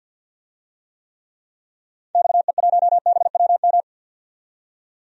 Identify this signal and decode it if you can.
Morse code, heard as a keyed tone, about 51 wpm, 700 Hz. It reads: XE1BWM
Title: DXCC-100 Call Signs - 6 Characters - 50wpm